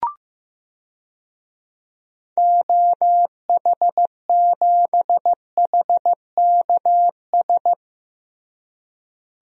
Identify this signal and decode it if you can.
Morse code, heard as a keyed tone, about 15 wpm, 700 Hz. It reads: OH7HKS
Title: DXCC-100 Call Signs - 6 Characters - Repeated 3x - 15wpm